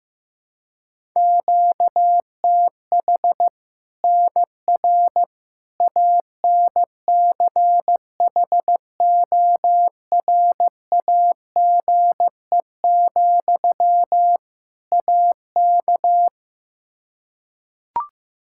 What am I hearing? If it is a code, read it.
Morse code, heard as a keyed tone, about 16 wpm, 700 Hz. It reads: QTH NR ANCHORAGE, AK
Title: K1USN SST QSO Practice 15wpm